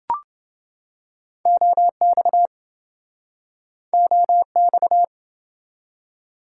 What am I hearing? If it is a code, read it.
Morse code, heard as a keyed tone, about 28 wpm, 700 Hz. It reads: OX OX